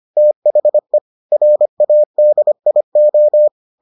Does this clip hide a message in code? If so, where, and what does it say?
Yes: Morse code from 0.2 to 3.5 s, heard as a keyed tone, about 25 wpm, 600 Hz. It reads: THE RADIO